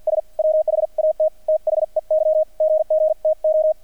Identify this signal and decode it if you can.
Morse code, heard as a keyed tone, about 48 wpm, spaced as for 42 wpm, 630 Hz, with radio noise. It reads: S WHAT THEY GGTO
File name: archived / CRSnet